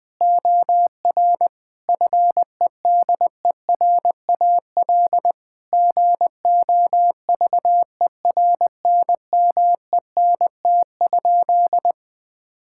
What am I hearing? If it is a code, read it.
Morse code, heard as a keyed tone, about 20 wpm, 700 Hz. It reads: OR FEDERAL GOVERNMENT?
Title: Sentences from Top 1000 Words - Review Part 5 - 20wpm